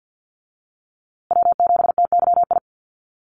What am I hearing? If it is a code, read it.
Morse code, heard as a keyed tone, about 50 wpm, 700 Hz. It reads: W7TYI
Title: USA Call Signs Repeated 2x 50wpm